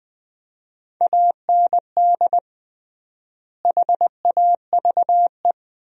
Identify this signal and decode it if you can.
Morse code, heard as a keyed tone, about 20 wpm, 700 Hz. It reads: AND HAVE
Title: 3x Word-Spacing - Sentences from Top 200 Words - 20wpm